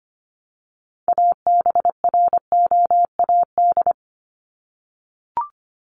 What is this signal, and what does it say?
Morse code, heard as a keyed tone, about 25 wpm, 700 Hz. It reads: ABROAD
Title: Top 1751-2000 Words 25wpm